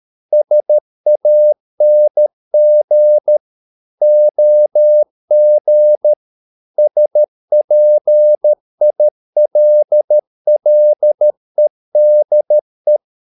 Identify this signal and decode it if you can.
Morse code, heard as a keyed tone, about 13 wpm, 600 Hz. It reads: SANG OG SPILLEDE
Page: Ferie 13 wpm